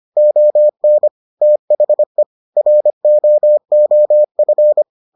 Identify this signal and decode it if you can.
Morse code, heard as a keyed tone, about 25 wpm, 600 Hz. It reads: ON THE ROOF